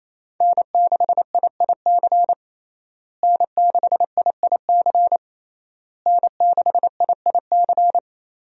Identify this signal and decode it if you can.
Morse code, heard as a keyed tone, about 28 wpm, 700 Hz. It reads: N6IIC N6IIC N6IIC
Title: USA Call Signs - 5 Characters - Repeated 3x - 28wpm